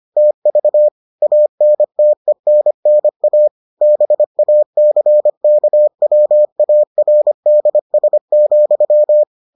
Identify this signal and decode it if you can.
Morse code, heard as a keyed tone, about 25 wpm, 600 Hz. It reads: TV ANTENNA BACKWARDS,